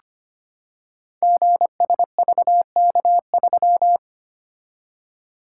Morse code, heard as a keyed tone, about 25 wpm, 700 Hz. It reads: GSVK3
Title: Code Groups - 5 Letter:Numbers - 25wpm